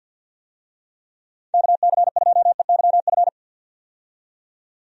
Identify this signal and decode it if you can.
Morse code, heard as a keyed tone, about 52 wpm, 700 Hz. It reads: KC2EYF
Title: USA Call Signs - 6 Characters - 50wpm